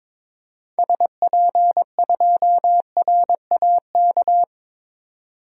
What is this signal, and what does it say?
Morse code, heard as a keyed tone, about 22 wpm, 700 Hz. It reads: SP2RAK
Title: DXCC-100 Call Signs - 6 Characters - 22wpm